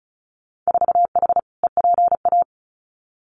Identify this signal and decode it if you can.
Morse code, heard as a keyed tone, about 36 wpm, 700 Hz. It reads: 4H EPA
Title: Winter Field Day Exchanges 35wpm